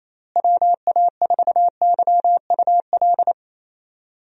Morse code, heard as a keyed tone, about 28 wpm, 700 Hz. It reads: WA4YUL
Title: USA Call Signs - 6 Characters - Repeated 3x - 28wpm